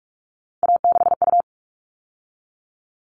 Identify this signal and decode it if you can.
Morse code, heard as a keyed tone, about 44 wpm, 700 Hz. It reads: A6U
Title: All Cut Numbers - 3-Digit 45wpm